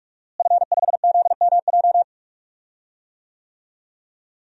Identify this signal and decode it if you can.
Morse code, heard as a keyed tone, about 46 wpm, 700 Hz. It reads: F5ZMJ
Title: DXCC-100 Call Signs - 5 Characters - Repeated 2x - 45wpm